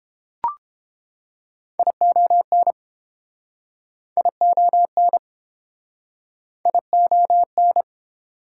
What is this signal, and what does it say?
Morse code, heard as a keyed tone, about 30 wpm, 700 Hz. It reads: ION ION ION